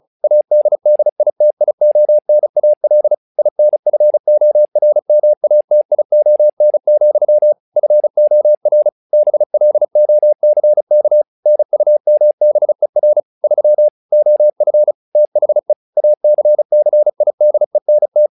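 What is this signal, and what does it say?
Morse code, heard as a keyed tone, about 37 wpm, 600 Hz. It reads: ADDITIONAL INFORMATION, FOR BLOCK NUMBER 3 OF THE ACCIDENT
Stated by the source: Morse practice text "Tower Trouble" sent at 35 WPM, from the SKCC - Straight Key Century Club learning center audio files.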